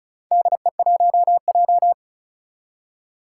Morse code, heard as a keyed tone, about 36 wpm, 700 Hz. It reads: DE1J
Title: Four Challenging Letter-Number Combinations 35wpm